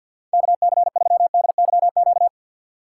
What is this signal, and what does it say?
Morse code, heard as a keyed tone, about 48 wpm, 700 Hz. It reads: KC3DYQ
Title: USA Call Signs - 6 Characters - 4x Character Spacing - 50wpm